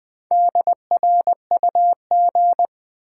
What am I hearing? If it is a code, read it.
Morse code, heard as a keyed tone, about 20 wpm, 700 Hz. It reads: DRUG
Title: Sets of 5 Words - New Word Drill - 20wpm